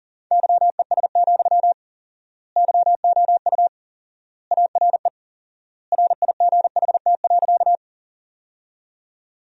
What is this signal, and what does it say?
Morse code, heard as a keyed tone, about 40 wpm, 700 Hz. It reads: YES, YOU ARE RIGHT.
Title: 4x Word-Spacing - Sentences from Top 500 Words - 40wpm